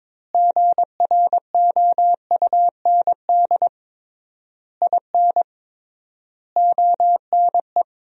Morse code, heard as a keyed tone, about 22 wpm, 700 Hz. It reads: GROUND IN ONE
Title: 3x Word-Spacing - Sentences from Top 500 Words - 22wpm